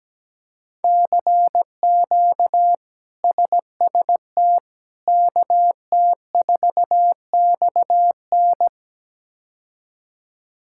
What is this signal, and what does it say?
Morse code, heard as a keyed tone, about 17 wpm, 700 Hz. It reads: CQ SST KT4XN
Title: K1USN SST QSO Practice - 17wpm